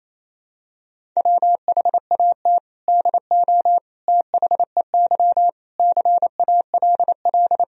Morse code, heard as a keyed tone, about 28 wpm, 700 Hz. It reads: WHAT DO THEY CALL
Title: Sentences from Top 100 Words and Abbreviations 28wpm